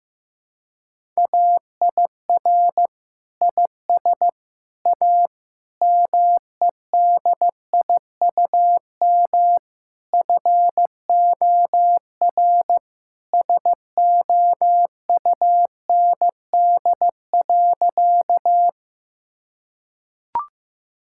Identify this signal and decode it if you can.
Morse code, heard as a keyed tone, about 14 wpm, 700 Hz. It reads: AIR IS A MEDIUM FOR SOUND.
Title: Sentences from Top 1000 Words - Review Part 6 - 15wpm